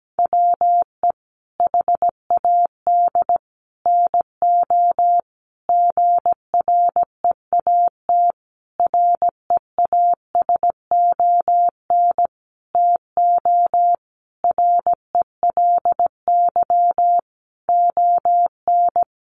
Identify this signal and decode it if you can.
Morse code, heard as a keyed tone, about 17 wpm, 700 Hz. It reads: WE HAD NO GREAT REASON TO RELY ON
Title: Sentences from Top 1200 Words 17wpm